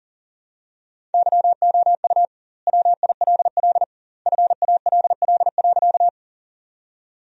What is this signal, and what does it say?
Morse code, heard as a keyed tone, about 40 wpm, 700 Hz. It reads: YOU WILL FALL.
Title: 2x Word-Spacing - Sentences from Top 500 Words - 40wpm